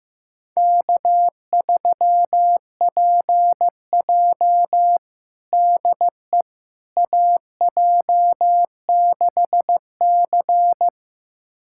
Morse code, heard as a keyed tone, about 15 wpm, 700 Hz. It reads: K3PJ DE AJ6C
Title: USA Callsign DE Callsign - 4 Characters - 15wpm